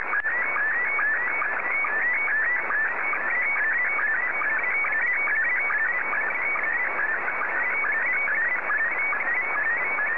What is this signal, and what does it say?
SSTV (slow-scan television), Scottie 1.